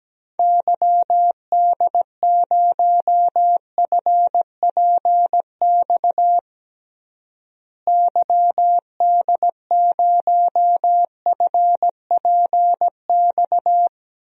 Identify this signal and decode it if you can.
Morse code, heard as a keyed tone, about 17 wpm, 700 Hz. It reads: YD0FPX YD0FPX
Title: DXCC-100 Call Signs - 6 Characters - Repeated 3x - 17wpm